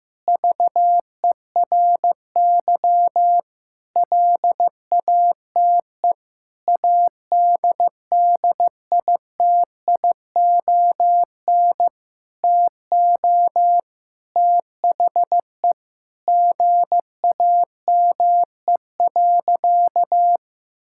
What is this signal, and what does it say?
Morse code, heard as a keyed tone, about 15 wpm, 700 Hz. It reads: VERY LATE ADDITION TO THE GAME.
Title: Sentences from Top 1000 Words - Review Part 6 - 15wpm